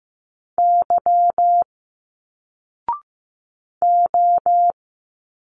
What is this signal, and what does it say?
Morse code, heard as a keyed tone, about 15 wpm, 700 Hz. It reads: Y O
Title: Single Letters 15wpm